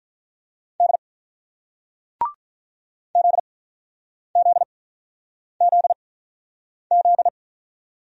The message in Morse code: D ZZZZ

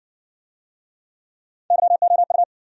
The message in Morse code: YOU

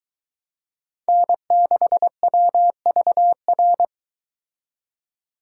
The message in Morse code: N6WVR

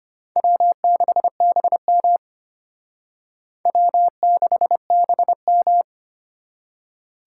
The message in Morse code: W6BM W6BM

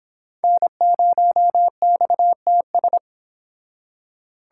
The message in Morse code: N0XTS

